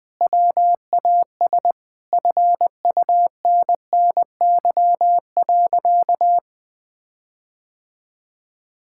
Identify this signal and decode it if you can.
Morse code, heard as a keyed tone, about 20 wpm, 700 Hz. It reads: WAS FUNNY.
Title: Sentences with 4-Word Phrases - 20wpm